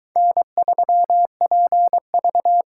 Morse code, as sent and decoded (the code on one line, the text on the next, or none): -. ...-- .--. ...-
N3PV